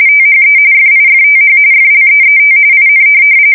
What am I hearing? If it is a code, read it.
RTTY, about 45.45 baud. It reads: THE FIFTH FLEET IN
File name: fsk170 / USS-New-Jersey